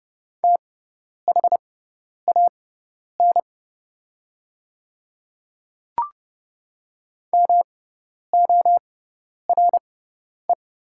Morse code, heard as a keyed tone, about 30 wpm, spaced as for 10 wpm, 700 Hz. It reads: THAN MORE
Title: 6x Character Spacing - Top 100 Words - 30wpm